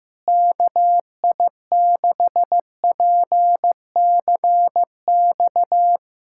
Morse code, heard as a keyed tone, about 16 wpm, 700 Hz. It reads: KI6PCX
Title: USA Callsign DE Callsign - 6 Characters - 15wpm